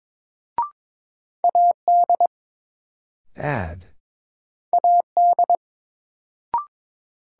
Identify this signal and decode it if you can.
Morse code, heard as a keyed tone, about 22 wpm, 700 Hz. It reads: AD AD
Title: Sentences from Top 1300 Words - New Word Drill - 22wpm